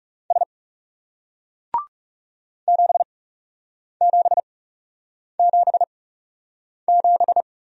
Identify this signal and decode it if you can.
Morse code, heard as a keyed tone, about 40 wpm, spaced as for 8 wpm, 700 Hz. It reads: S 7777